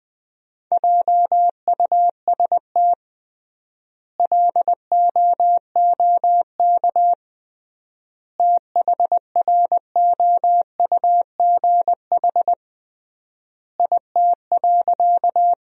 Morse code, heard as a keyed tone, about 21 wpm, 700 Hz. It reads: JUST LOOK THROUGH IT.
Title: 3x Word-Spacing - Sentences from Top 200 Words - 20wpm